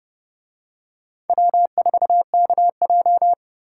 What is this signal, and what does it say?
Morse code, heard as a keyed tone, about 29 wpm, 700 Hz. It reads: W4KJ